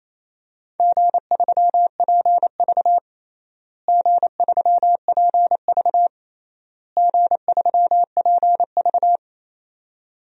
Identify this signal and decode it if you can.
Morse code, heard as a keyed tone, about 28 wpm, 700 Hz. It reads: G3PV G3PV G3PV
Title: DXCC-100 Call Signs - 4 Characters - Repeated 3x - 28wpm